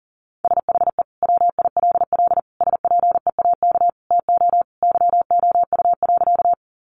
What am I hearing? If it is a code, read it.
Morse code, heard as a keyed tone, about 40 wpm, 700 Hz. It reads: SHE WILL SPEAK TO YOU.
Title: Sentences from Top 1100 Words 40wpm